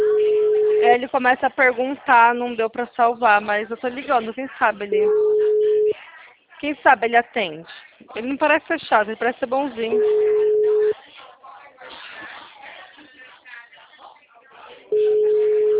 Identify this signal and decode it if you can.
Ringback tone (Europe, 425 Hz), with speech.
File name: out- -2032